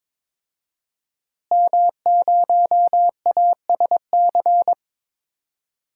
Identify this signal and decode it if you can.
Morse code, heard as a keyed tone, about 22 wpm, 700 Hz. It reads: M0ASC
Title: DXCC-100 Call Signs - 5 Characters - Repeated 2x - 22wpm